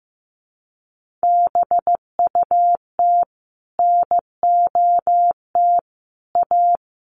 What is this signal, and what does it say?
Morse code, heard as a keyed tone, about 15 wpm, 700 Hz. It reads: BUT NOT A
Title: Sentences from Top 1400 Words 15wpm